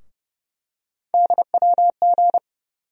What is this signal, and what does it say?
Morse code, heard as a keyed tone, about 31 wpm, 700 Hz. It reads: DWG
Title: USA Call Sign Postfixes - 3 Letters 30wpm